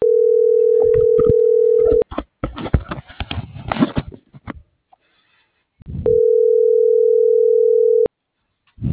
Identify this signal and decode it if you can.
Ringback tone (North America), 440 + 480 Hz.